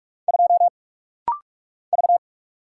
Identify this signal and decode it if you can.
Morse code, heard as a keyed tone, about 44 wpm, 700 Hz. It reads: 2 V
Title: Single Letter-Number Alt - Rapid-Fire 45wpm